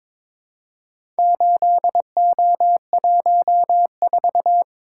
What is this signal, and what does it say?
Morse code, heard as a keyed tone, about 22 wpm, 700 Hz. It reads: 8O14
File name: practice / 22wpm / Four Letter-Number Combinations - 22wpm